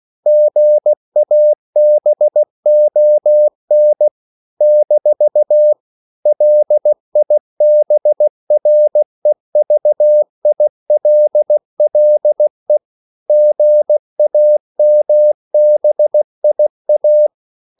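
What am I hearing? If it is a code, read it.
Morse code, heard as a keyed tone, about 16 wpm, 600 Hz. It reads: GABON - LIBREVILLE GAMBIA
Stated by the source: CW practice of national capitals at 16 words per minute, from ON6ZQ.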